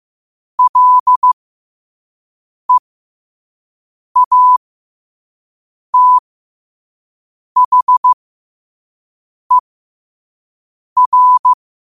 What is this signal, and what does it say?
Morse code, heard as a keyed tone, about 15 wpm, spaced as for 5 wpm, 1000 Hz. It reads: LEATHER
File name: qotdfiles / 5wpm